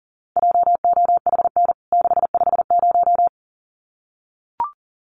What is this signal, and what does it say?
Morse code, heard as a keyed tone, about 40 wpm, 700 Hz. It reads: JOHN 650
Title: CWT Exchange 40wpm